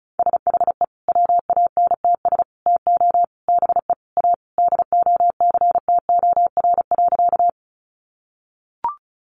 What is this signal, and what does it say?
Morse code, heard as a keyed tone, about 34 wpm, 700 Hz. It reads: SHE WANTS TO BE A DOCTOR.
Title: Easy Phrases 35wpm